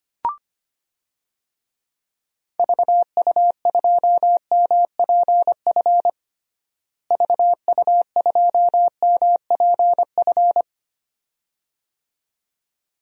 Morse code, heard as a keyed tone, about 25 wpm, 700 Hz. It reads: VU2MPF VU2MPF
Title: DXCC-100 Call Signs - 6 Characters - Repeated 2x - 25wpm